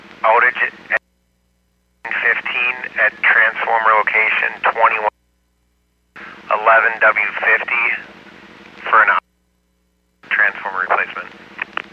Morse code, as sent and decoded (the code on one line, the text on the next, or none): none